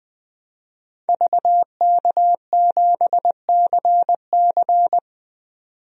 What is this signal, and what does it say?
Morse code, heard as a keyed tone, about 20 wpm, 700 Hz. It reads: VK7CC
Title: DXCC-100 Call Signs - 5 Characters - 20wpm